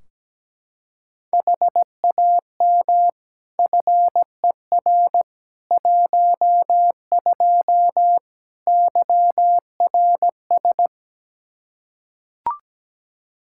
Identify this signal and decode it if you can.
Morse code, heard as a keyed tone, about 17 wpm, 700 Hz. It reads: HAM FER 12 YRS
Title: Practice QSO - Exchange 3 - More Info - 17wpm